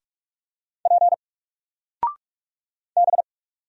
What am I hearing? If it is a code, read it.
Morse code, heard as a keyed tone, about 44 wpm, 700 Hz. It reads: P B
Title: Single Character (Letter:Number:Punctuation:Prosign) 45wpm